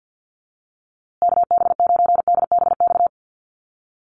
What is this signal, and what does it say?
Morse code, heard as a keyed tone, about 50 wpm, 700 Hz. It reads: KB9DBX
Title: USA Call Signs Repeated 3x 50wpm